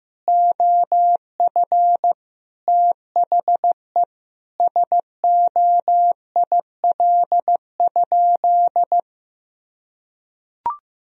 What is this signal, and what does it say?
Morse code, heard as a keyed tone, about 15 wpm, 700 Hz. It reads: OF THE SOIL?
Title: Sentences from Top 1000 Words - Review Part 4 - 15wpm